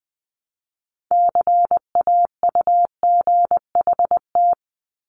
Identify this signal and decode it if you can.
Morse code, heard as a keyed tone, about 20 wpm, 700 Hz. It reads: CAUGHT
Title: Multi-letter Phonemes 20wpm